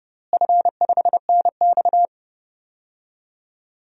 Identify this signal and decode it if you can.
Morse code, heard as a keyed tone, about 30 wpm, 700 Hz. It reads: F5NX
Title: DXCC-100 Call Signs - 4 Characters - Repeated 3x - 30wpm